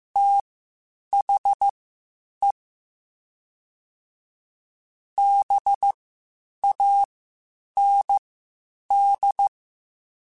Morse code, heard as a keyed tone, about 15 wpm, spaced as for 8 wpm, 800 Hz. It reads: THE BAND